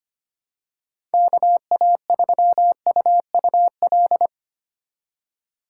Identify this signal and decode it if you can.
Morse code, heard as a keyed tone, about 25 wpm, 700 Hz. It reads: KA3UUL
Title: USA Call Signs - 6 Characters - 2x Character Spacing - 25wpm